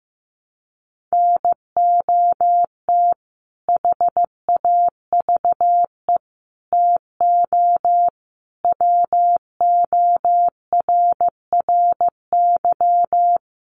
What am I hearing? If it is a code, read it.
Morse code, heard as a keyed tone, about 15 wpm, 700 Hz. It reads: NOT HAVE TO WORRY
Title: Sets of 5 Words 15wpm